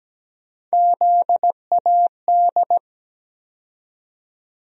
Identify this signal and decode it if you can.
Morse code, heard as a keyed tone, about 17 wpm, 700 Hz. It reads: ZAD
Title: USA Call Sign Postfixes - 3 Letters 17wpm